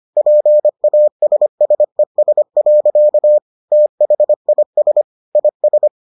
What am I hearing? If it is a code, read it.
Morse code, heard as a keyed tone, about 25 wpm, 600 Hz. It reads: PASSES. THIS IS